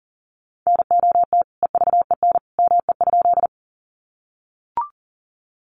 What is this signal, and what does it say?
Morse code, heard as a keyed tone, about 40 wpm, 700 Hz. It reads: NOT EVEN ME?
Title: Sentences from Top 600 Words 40wpm